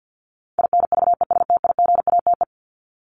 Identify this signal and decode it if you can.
Morse code, heard as a keyed tone, about 50 wpm, 700 Hz. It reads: INVESTIGATE
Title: Top 1751-2000 Words 50wpm